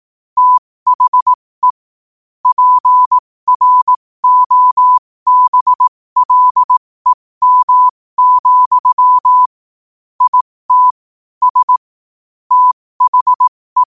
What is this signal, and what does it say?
Morse code, heard as a keyed tone, about 18 wpm, spaced as for 15 wpm, 1000 Hz. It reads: THE PROBLEM, IT S THE
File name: qotdfiles / 15wpm